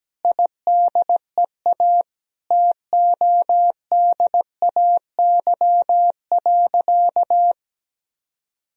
Morse code, heard as a keyed tone, about 17 wpm, 700 Hz. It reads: IDEA TODAY.
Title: Sentences from Top 1000 Words - Review Part 4 - 17wpm